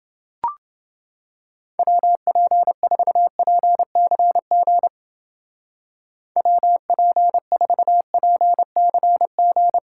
Morse code, heard as a keyed tone, about 31 wpm, 700 Hz. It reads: WP4PCG WP4PCG